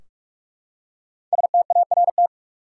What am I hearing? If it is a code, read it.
Morse code, heard as a keyed tone, about 41 wpm, 700 Hz. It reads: START
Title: Five Letter Words - 45wpm